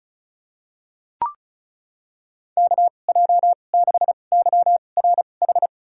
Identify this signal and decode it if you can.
Morse code, heard as a keyed tone, about 36 wpm, spaced as for 25 wpm, 690 Hz. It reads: KJ6YRH
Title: USA Call Signs - 6 Characters - 2x Character Spacing - 35wpm